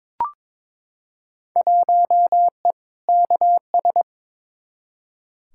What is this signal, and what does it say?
Morse code, heard as a keyed tone, about 22 wpm, 700 Hz. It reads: 1E KS
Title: Field Day Exchanges - 22wpm